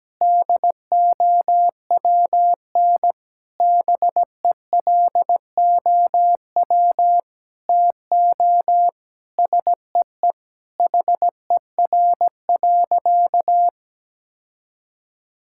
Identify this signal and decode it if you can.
Morse code, heard as a keyed tone, about 17 wpm, 700 Hz. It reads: DOWN BELOW TO SEE HER.
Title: Sentences from Top 1000 Words - Review Part 5 - 17wpm